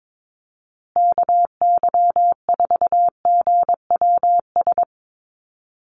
Morse code, heard as a keyed tone, about 22 wpm, 700 Hz. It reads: KY4GWS
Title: USA Call Signs Repeated 3x 22wpm